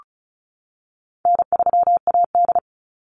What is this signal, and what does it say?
Morse code, heard as a keyed tone, about 34 wpm, 700 Hz. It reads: N3AD
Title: Call Signs - 35wpm